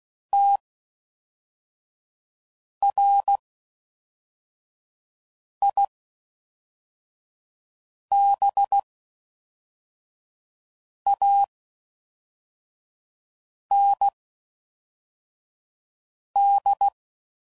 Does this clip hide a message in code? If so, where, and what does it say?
Yes: Morse code from 0.3 to 16.9 s, heard as a keyed tone, about 16 wpm, spaced as for 4 wpm, 800 Hz. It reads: TRIBAND